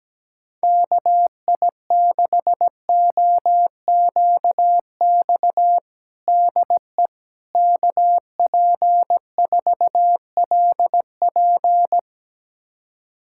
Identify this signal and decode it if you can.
Morse code, heard as a keyed tone, about 17 wpm, 700 Hz. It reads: KI6OQX DE KP4LP
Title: USA Callsign DE Callsign 17wpm